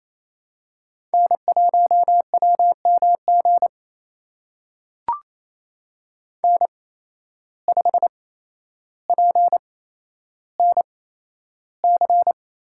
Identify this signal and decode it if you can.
Morse code, heard as a keyed tone, about 28 wpm, spaced as for 12 wpm, 700 Hz. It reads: N1WMG N5PNC
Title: USA Call Signs - 5 Characters - 8x Character Spacing - 28wpm